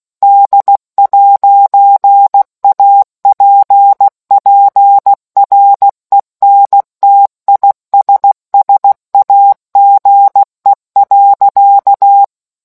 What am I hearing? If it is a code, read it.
Morse code, heard as a keyed tone, about 16 wpm, 800 Hz. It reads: D'APPRENTISSAGE.